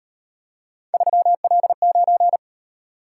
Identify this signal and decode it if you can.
Morse code, heard as a keyed tone, about 38 wpm, 700 Hz. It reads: 3L9